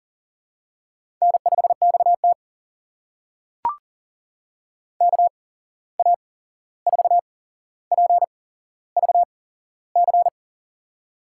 Morse code, heard as a keyed tone, about 42 wpm, spaced as for 15 wpm, 700 Hz. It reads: N5XT KA4PVC
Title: USA Call Signs - 8x Character Spacing - 40wpm